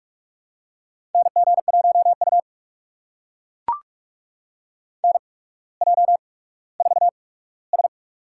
Morse code, heard as a keyed tone, about 44 wpm, spaced as for 19 wpm, 700 Hz. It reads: NG1U NJ4S
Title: USA Call Signs - 4 Characters - 8x Character Spacing - 45wpm